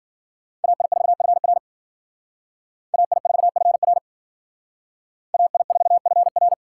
Morse code, heard as a keyed tone, about 50 wpm, 700 Hz. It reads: AI4FR AI4FR AI4FR
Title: Speed Racing - USA Call Signs 40wpm